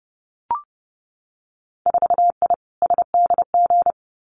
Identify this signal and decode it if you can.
Morse code, heard as a keyed tone, about 30 wpm, 700 Hz. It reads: 4I SDG
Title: Winter Field Day Exchanges Encore 30wpm